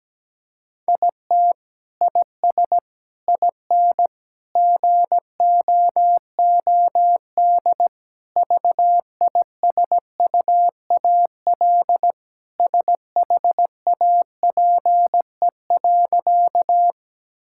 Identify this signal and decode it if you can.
Morse code, heard as a keyed tone, about 17 wpm, 700 Hz. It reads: IT IS IN GOOD VISUAL SHAPE.
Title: Sentences from Top 1000 Words - Review Part 6 - 17wpm